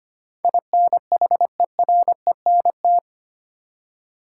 Morse code, heard as a keyed tone, about 25 wpm, 700 Hz. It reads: INHERENT
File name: Eight Letter Words - 25wpm